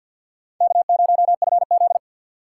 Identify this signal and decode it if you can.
Morse code, heard as a keyed tone, about 52 wpm, 700 Hz. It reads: K0FZ
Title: USA Call Signs - 4 Characters - 3x Character Spacing - 50wpm